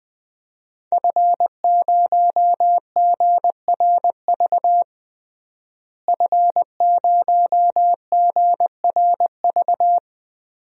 Morse code, heard as a keyed tone, about 20 wpm, 700 Hz. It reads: F0GRV F0GRV
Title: DXCC-100 Call Signs - 5 Characters - Repeated 2x - 20wpm